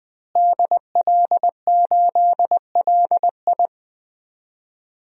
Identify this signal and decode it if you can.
Morse code, heard as a keyed tone, about 19 wpm, 700 Hz. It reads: DL8LI
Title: DXCC-100 Call Signs - 5 Characters - Repeated 3x - 20wpm